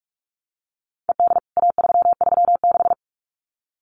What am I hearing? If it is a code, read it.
Morse code, heard as a keyed tone, about 44 wpm, 700 Hz. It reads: ED A336
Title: CWT Exchange 45wpm